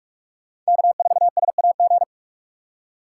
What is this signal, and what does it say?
Morse code, heard as a keyed tone, about 47 wpm, 690 Hz. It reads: K4SAG
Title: USA Call Signs - 5 Characters - 2x Character Spacing - 45wpm